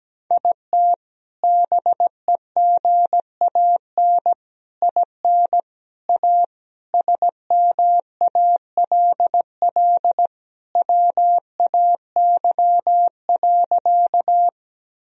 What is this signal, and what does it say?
Morse code, heard as a keyed tone, about 17 wpm, 700 Hz. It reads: IT BEGAN IN A SMALL WAY.
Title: Sentences from Top 1000 Words - Review Part 1 - 17wpm